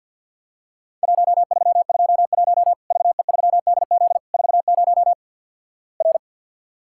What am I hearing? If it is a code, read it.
Morse code, heard as a keyed tone, about 50 wpm, 700 Hz. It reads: 1321 VE3DZ 40 R
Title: Worked All Europe DX Contest - QTC Exchange - 50wpm